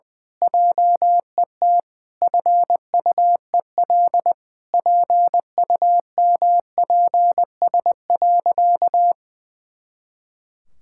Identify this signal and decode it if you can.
Morse code, heard as a keyed tone, about 20 wpm, 700 Hz. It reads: JET FUEL PUMPS.
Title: Pangram Challenge v2 - 20wpm